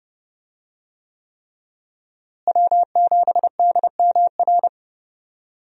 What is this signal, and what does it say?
Morse code, heard as a keyed tone, about 30 wpm, 700 Hz. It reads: W7DMR
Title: USA Call Signs - 5 Characters - Repeated 3x - 30wpm